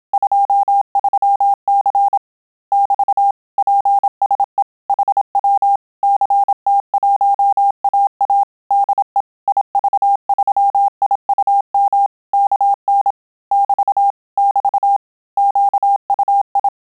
Morse code, heard as a keyed tone, about 27 wpm, 800 Hz. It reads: 23C <BT> PSE HW CT1AA DE IV3IUM KN <BT> <BT> QUI
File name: Telegraphy-group / QSO CW 25 PPM